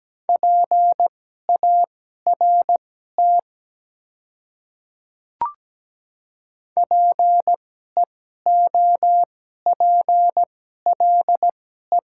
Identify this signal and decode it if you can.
Morse code, heard as a keyed tone, about 17 wpm, spaced as for 12 wpm, 700 Hz. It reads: PART PEOPLE
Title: 2x Character Spacing - Top 200 Words - 17wpm